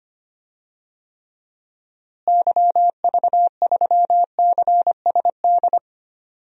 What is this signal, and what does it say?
Morse code, heard as a keyed tone, about 25 wpm, 700 Hz. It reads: YV3CSD
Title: DXCC-100 Call Signs - 6 Characters - Repeated 2x - 25wpm